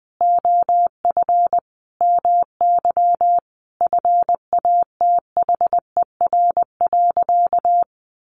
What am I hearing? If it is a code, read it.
Morse code, heard as a keyed tone, about 20 wpm, 700 Hz. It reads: OF MY FATHER.
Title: Sentences from Top 500 Words 20wpm